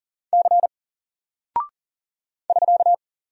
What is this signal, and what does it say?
Morse code, heard as a keyed tone, about 39 wpm, 700 Hz. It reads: C <SK>
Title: Single Character (Letter:Number:Punctuation:Prosign) 40wpm